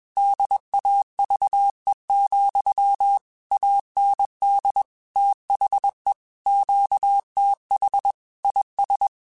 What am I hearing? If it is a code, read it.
Morse code, heard as a keyed tone, about 21 wpm, 800 Hz. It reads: DAVE, AND THE QTH IS
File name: QSOs / QSOs-mp3s / kc7haa@20